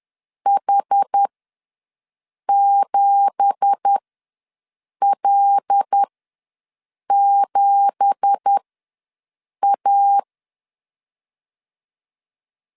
Morse code, heard as a keyed tone, about 10 wpm, spaced as for 6 wpm, 800 Hz. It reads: H7L7A
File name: LEZIONI / 9 2867 ws 10wpm 4215